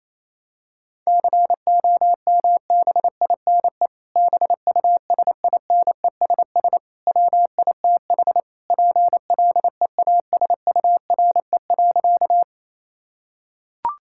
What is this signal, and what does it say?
Morse code, heard as a keyed tone, about 28 wpm, 700 Hz. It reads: COMBINE BUSINESS WITH PLEASURE.
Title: Sentences from Top 1600 Words 28wpm